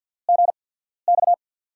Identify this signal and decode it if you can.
Morse code, heard as a keyed tone, about 53 wpm, spaced as for 14 wpm, 700 Hz. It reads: GX
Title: USA Call Signs - 6 Characters - 8x Character Spacing - 50wpm